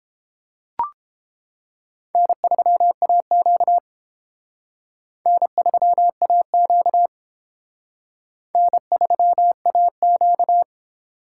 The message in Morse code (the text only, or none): N3AQ N3AQ N3AQ